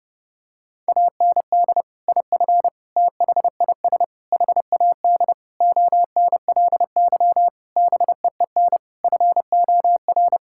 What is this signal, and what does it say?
Morse code, heard as a keyed tone, about 30 wpm, 700 Hz. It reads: AND IF THIS HAD ONLY BEEN FOR
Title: Sentences from Top 1600 Words 30wpm